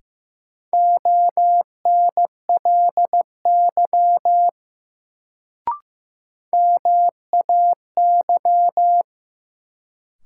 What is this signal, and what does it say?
Morse code, heard as a keyed tone, about 15 wpm, 700 Hz. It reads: ONLY MAY
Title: Sets of 2 Words - New Word Drill - 15wpm